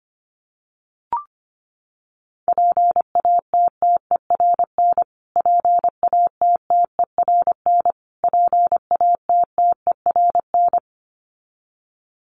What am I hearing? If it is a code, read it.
Morse code, heard as a keyed tone, about 25 wpm, 700 Hz. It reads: PATTERN PATTERN PATTERN
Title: Top 500 Words Repeated 25wpm